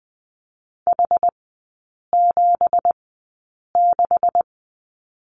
Morse code, heard as a keyed tone, about 20 wpm, spaced as for 8 wpm, 700 Hz. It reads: H76
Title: 198 Lesson 32e - Letter k - Three Characters 20wpm